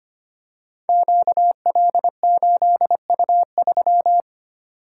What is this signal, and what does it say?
Morse code, heard as a keyed tone, about 25 wpm, 700 Hz. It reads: QL8U3
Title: Code Groups - 5 Letter:Numbers - 25wpm